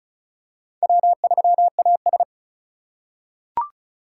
Morse code, heard as a keyed tone, about 36 wpm, 700 Hz. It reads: W3AS
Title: USA Call Signs - 4 Characters - 3x Character Spacing - 35wpm